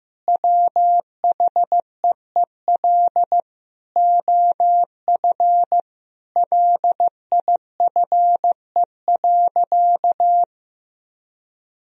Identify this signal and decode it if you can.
Morse code, heard as a keyed tone, about 15 wpm, 700 Hz. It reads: WHEEL OF LIFE.
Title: Sentences from Top 1000 Words - Review Part 3 - 15wpm